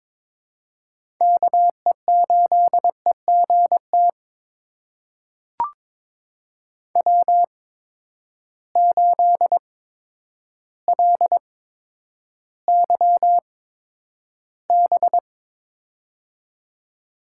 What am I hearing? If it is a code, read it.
Morse code, heard as a keyed tone, about 21 wpm, spaced as for 10 wpm, 700 Hz. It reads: KE8EGT W8LYB
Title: USA Call Signs - 8x Character Spacing - 22wpm